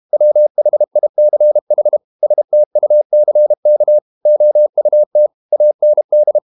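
Morse code, heard as a keyed tone, about 32 wpm, 600 Hz. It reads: WHICH STUCK OUT AND
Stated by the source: Morse code practice at 32 wpm from CW med Gnister, reading Ken Follett's novel Never in English.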